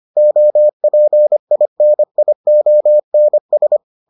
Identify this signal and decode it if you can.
Morse code, heard as a keyed tone, about 25 wpm, 600 Hz. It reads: OPINIONS